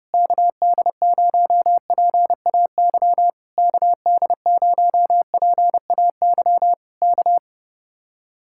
Morse code, heard as a keyed tone, about 33 wpm, 700 Hz. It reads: KD0PAY KD0PAY K